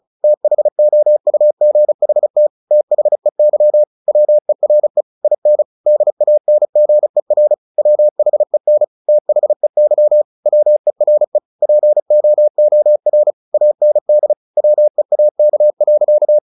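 Morse code, heard as a keyed tone, about 34 wpm, 600 Hz. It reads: THOUGHT THEY WERE IN DANGER WHEN THEY WERE POOR AND WEAK.
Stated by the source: Morse practice at 35 wpm from CW med Gnister, sending Ken Follett's novel Never in English.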